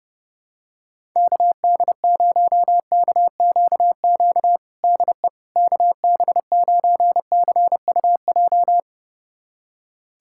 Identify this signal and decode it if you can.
Morse code, heard as a keyed tone, about 32 wpm, 700 Hz. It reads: KD0KQQ DE KB9CUJ
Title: USA Callsign DE Callsign - 6 Characters - 30wpm